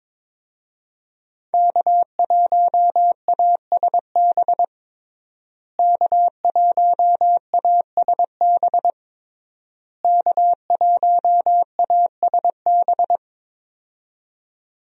Morse code, heard as a keyed tone, about 22 wpm, 700 Hz. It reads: K1ASB K1ASB K1ASB
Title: USA Call Signs - 5 Characters - Repeated 3x - 22wpm